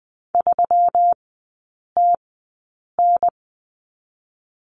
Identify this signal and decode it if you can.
Morse code, heard as a keyed tone, about 20 wpm, 690 Hz. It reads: 3 T N
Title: 145 Lesson 24e - Letter y - Three Characters 20wpm